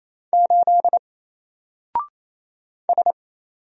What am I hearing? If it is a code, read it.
Morse code, heard as a keyed tone, about 27 wpm, 700 Hz. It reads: 8 S
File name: Single-Letter-Number-28wpm